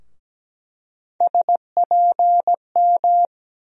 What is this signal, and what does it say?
Morse code, heard as a keyed tone, about 17 wpm, 700 Hz. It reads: SPM
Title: USA Call Sign Postfixes - 3 Letters 17wpm